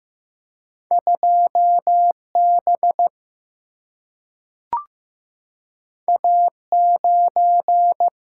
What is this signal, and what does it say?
Morse code, heard as a keyed tone, about 15 wpm, 700 Hz. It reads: 2B A9
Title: Single-Letter Single-Number 15wpm